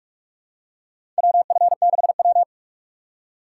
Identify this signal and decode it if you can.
Morse code, heard as a keyed tone, about 43 wpm, 700 Hz. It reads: WF6W
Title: USA Call Signs - 4 Characters - 2x Character Spacing - 45wpm